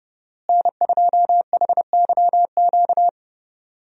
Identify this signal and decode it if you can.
Morse code, heard as a keyed tone, about 32 wpm, 700 Hz. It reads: N2HYQ
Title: USA Call Signs - 5 Characters - Repeated 2x - 30wpm